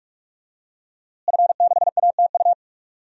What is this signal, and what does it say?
Morse code, heard as a keyed tone, about 47 wpm, 700 Hz. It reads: F6ATU
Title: DXCC-100 Call Signs - 5 Characters - Repeated 3x - 45wpm